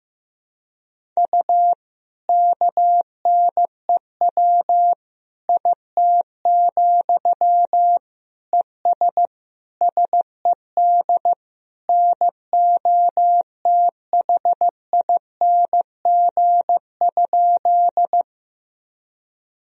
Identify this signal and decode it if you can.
Morse code, heard as a keyed tone, about 15 wpm, 700 Hz. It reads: U KNEW IT, ES SED NOTHING?
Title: Sentences from Top 500 Words and Abbreviations 15wpm